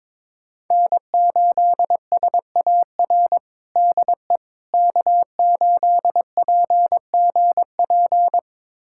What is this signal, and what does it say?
Morse code, heard as a keyed tone, about 21 wpm, 700 Hz. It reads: N8SAR DE K8PGP
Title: USA Callsign DE Callsign - 5 Characters - 22wpm